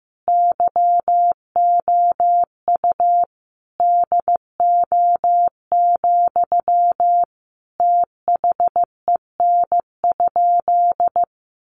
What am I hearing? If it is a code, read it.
Morse code, heard as a keyed tone, about 15 wpm, 700 Hz. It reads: YOU DO, THEN?
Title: Sentences from Top 100 Words 15wpm